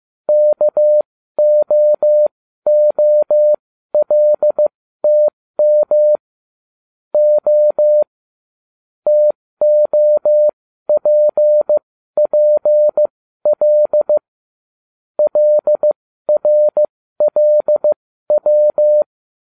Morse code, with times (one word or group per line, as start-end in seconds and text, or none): 0.3-6.2 KOOLTM
7.1-8.0 O
9.1-14.2 TOPPL
15.2-19.0 LRLW